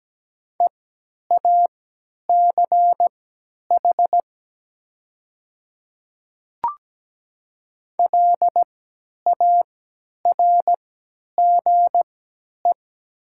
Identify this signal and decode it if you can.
Morse code, heard as a keyed tone, about 17 wpm, spaced as for 10 wpm, 700 Hz. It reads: EACH LARGE
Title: 3x Character Spacing - Top 200 Words - 17wpm